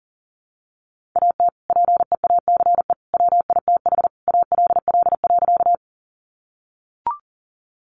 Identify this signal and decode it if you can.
Morse code, heard as a keyed tone, about 40 wpm, 700 Hz. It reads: AT PEACE WITH ALL.
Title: Sentences from Top 1500 Words 40wpm